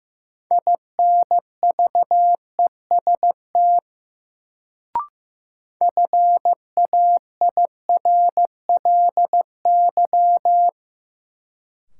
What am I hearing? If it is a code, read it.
Morse code, heard as a keyed tone, about 15 wpm, 700 Hz. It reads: INVEST FAIRLY
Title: Sentences from Top 1100 Words - New Word Drill Alt - 15wpm